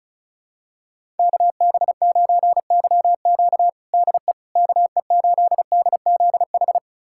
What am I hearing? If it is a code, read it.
Morse code, heard as a keyed tone, about 36 wpm, 700 Hz. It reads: KB9YQ DE KE8DZH
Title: USA Callsign DE Callsign 35wpm